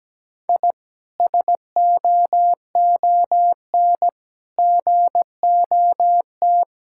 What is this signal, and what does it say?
Morse code, heard as a keyed tone, about 17 wpm, 700 Hz. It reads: I SOON GOT
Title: Sentences from Top 1000 Words - Review Part 6 - 17wpm